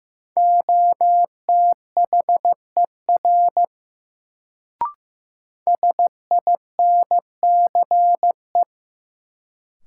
Morse code, heard as a keyed tone, about 15 wpm, 700 Hz. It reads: OTHER SINCE
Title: Sets of 2 Words - New Word Drill - 15wpm